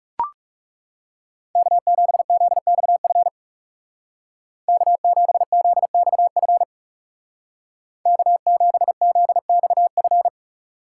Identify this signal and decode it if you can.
Morse code, heard as a keyed tone, about 40 wpm, 700 Hz. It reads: K7ZXF K7ZXF K7ZXF